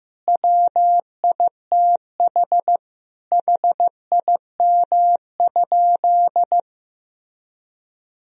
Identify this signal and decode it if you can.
Morse code, heard as a keyed tone, about 14 wpm, 700 Hz. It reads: WITH HIM?
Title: Sentences from Top 1000 Words - Review Part 3 - 15wpm